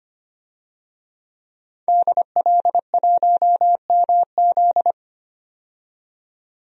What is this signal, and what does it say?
Morse code, heard as a keyed tone, about 25 wpm, 700 Hz. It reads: DL1MZ
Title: DXCC-100 Call Signs - 5 Characters - 25wpm